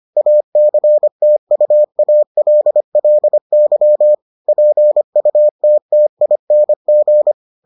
Morse code, heard as a keyed tone, about 25 wpm, 600 Hz. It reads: ACTUALLY PUTTING